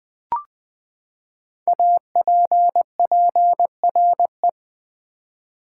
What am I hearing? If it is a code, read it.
Morse code, heard as a keyed tone, about 20 wpm, 700 Hz. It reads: APPRE
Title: Build A Word - 5 Letters At A Time - 12-Letter Words - 20wpm